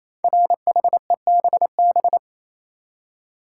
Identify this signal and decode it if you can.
Morse code, heard as a keyed tone, about 29 wpm, 700 Hz. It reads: RHEBB
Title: Code Groups - 5 Letters - 28wpm